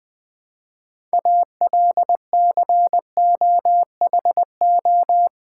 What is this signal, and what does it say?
Morse code, heard as a keyed tone, about 20 wpm, 700 Hz. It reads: ALCOHO
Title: Build A Word - 3 Letters At A Time - 9-Letter Words - 20wpm